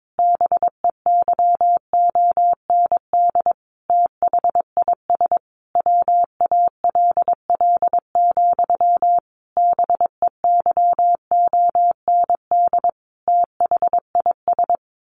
Morse code, heard as a keyed tone, about 22 wpm, 700 Hz. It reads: BEYOND THIS WALL, BEYOND THIS
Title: A Sound Of Thunder 22wpm